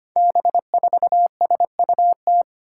Morse code, heard as a keyed tone, about 25 wpm, 700 Hz. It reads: B4SUT
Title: Five Challenging Letter-Number Combinations 25wpm